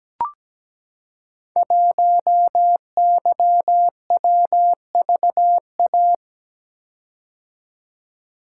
Morse code, heard as a keyed tone, about 17 wpm, 700 Hz. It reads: 1YWVA
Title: Code Groups - 5 Letter:Numbers - 17wpm